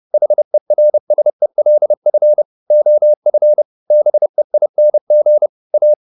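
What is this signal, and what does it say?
Morse code, heard as a keyed tone, about 30 wpm, 600 Hz. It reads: HERSELF OF BEING A
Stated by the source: Morse code at 30 wpm, the novel Never by Ken Follett in English, from CW med Gnister.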